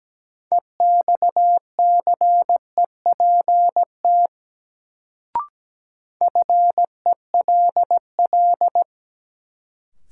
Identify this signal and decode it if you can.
Morse code, heard as a keyed tone, about 17 wpm, 700 Hz. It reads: EXCEPT FELL
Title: Sentences from Top 1600 Words - New Word Drill - 17wpm